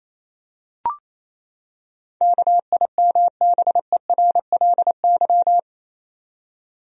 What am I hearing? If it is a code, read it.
Morse code, heard as a keyed tone, about 28 wpm, 700 Hz. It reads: KIMBERLY
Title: USA Common Names - 28wpm